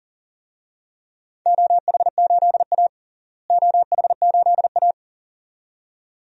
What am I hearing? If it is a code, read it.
Morse code, heard as a keyed tone, about 37 wpm, 700 Hz. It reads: OH8A OH8A
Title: DXCC-100 Call Signs - 4 Characters - Repeated 2x - 40wpm